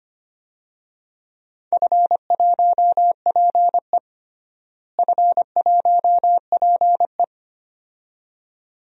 Morse code, heard as a keyed tone, about 25 wpm, 700 Hz. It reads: F1PE F1PE
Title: DXCC-100 Call Signs - 4 Characters - Repeated 2x - 25wpm